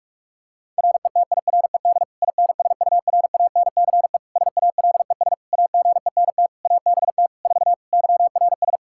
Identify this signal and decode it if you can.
Morse code, heard as a keyed tone, about 41 wpm, 700 Hz. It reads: RETIRED INSURANCE SALES AGENT ABT 4 YRS
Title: K1USN SST QSO Practice 45wpm